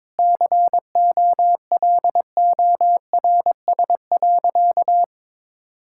Morse code, heard as a keyed tone, about 22 wpm, 700 Hz. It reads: COLORS.
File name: practice / 22wpm / Sentences with 4-Word Phrases - 22wpm